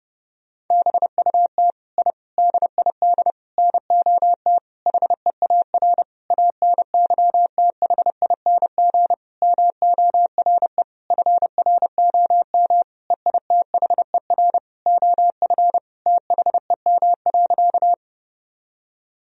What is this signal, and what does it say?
Morse code, heard as a keyed tone, about 30 wpm, 700 Hz. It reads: BUT I DID NOT HEAR ANYTHING MORE FROM EITHER OF THEM.
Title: Sentences from Top 1000 Words - Review Part 4 - 30wpm